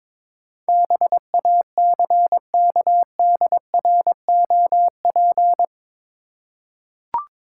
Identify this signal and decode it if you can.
Morse code, heard as a keyed tone, about 22 wpm, 700 Hz. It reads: BACKDROP